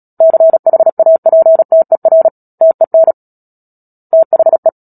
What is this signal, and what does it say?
Morse code, heard as a keyed tone, about 37 wpm, 650 Hz. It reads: CHAPTER TEN THE
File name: Book Two / 10-Chapter 10 - Epilogue - 36 WPM